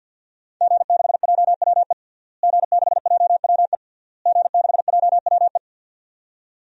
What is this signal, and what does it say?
Morse code, heard as a keyed tone, about 51 wpm, 700 Hz. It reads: G6JWE G6JWE G6JWE
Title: DXCC-100 Call Signs - 5 Characters - Repeated 3x - 50wpm